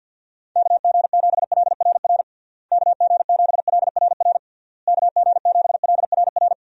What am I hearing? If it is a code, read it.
Morse code, heard as a keyed tone, about 48 wpm, 700 Hz. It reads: KG7LRR KG7LRR KG7LRR
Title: USA Call Signs - 6 Characters - Repeated 3x - 50wpm